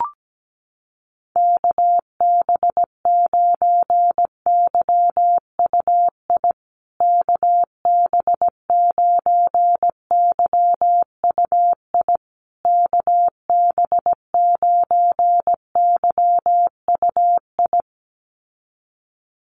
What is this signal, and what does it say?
Morse code, heard as a keyed tone, about 17 wpm, 700 Hz. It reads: KB9YUI KB9YUI KB9YUI
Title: USA Call Signs Repeated 3x 17wpm